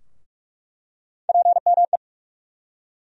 Morse code, heard as a keyed tone, about 46 wpm, 700 Hz. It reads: PME
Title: USA Call Sign Postfixes - 3 Letters 45wpm